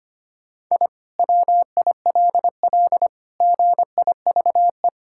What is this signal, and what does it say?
Morse code, heard as a keyed tone, about 25 wpm, 700 Hz. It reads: I WILL GIVE
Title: Sentences from Top 1000 Words - Review Part 6 - 25wpm